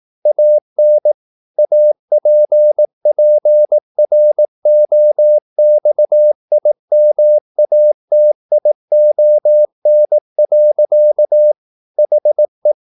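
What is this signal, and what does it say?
Morse code, heard as a keyed tone, about 18 wpm, 600 Hz. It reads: AN APPROXIMATION. HE
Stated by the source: Morse practice at 18 wpm, English text, from CW med Gnister.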